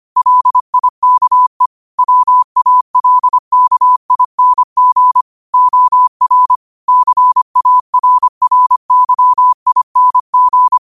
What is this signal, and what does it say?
Morse code, heard as a keyed tone, about 25 wpm, 1000 Hz. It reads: LIKE WALKING OR CARRYING